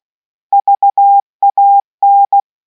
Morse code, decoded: VAN